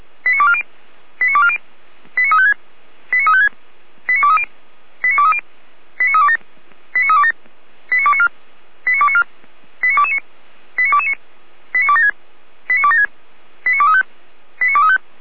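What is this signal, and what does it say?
Five-tone selective call (ZVEI), heard as quick runs of five tones, tones of about 70 ms. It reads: 89240 89240 89356 89356 89230 89230 89327 89327 89294 89294 89229 89229 89276 89276 89235 89235
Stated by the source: ZVEI five-tone selective call received by SDR, from the Feuerwehr Orlamünde audio archive.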